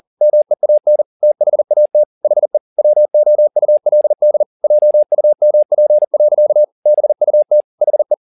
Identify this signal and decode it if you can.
Morse code, heard as a keyed tone, about 40 wpm, 600 Hz. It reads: MEAN THAT HE WOULD JUMP. BUT HE